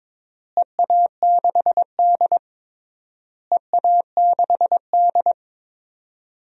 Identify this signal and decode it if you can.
Morse code, heard as a keyed tone, about 22 wpm, 700 Hz. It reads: EA6D EA6D
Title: DXCC-100 Call Signs - 4 Characters - Repeated 3x - 22wpm